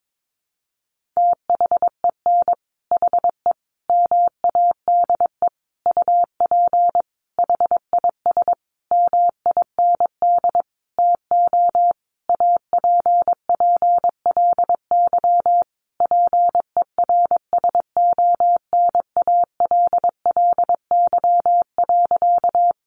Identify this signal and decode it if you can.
Morse code, heard as a keyed tone, about 22 wpm, 700 Hz. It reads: THEN HE MADE UP HIS MIND TO APPLY PERSONALLY.
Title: Sentences from Top 1400 Words 22wpm